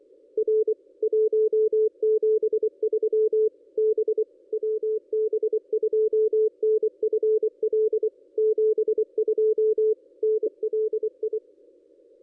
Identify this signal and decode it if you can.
Morse code, heard as a keyed tone, about 24 wpm, 440 Hz. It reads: R 173 B WB2NFL 72 NLI
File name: contests / ss-cw-k5tr-op-n5rz -1400